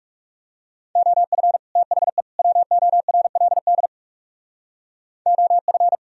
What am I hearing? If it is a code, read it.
Morse code, heard as a keyed tone, about 37 wpm, 700 Hz. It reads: OF THE WORLD OF
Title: Speed-Racing - Sets of 3 Words 30wpm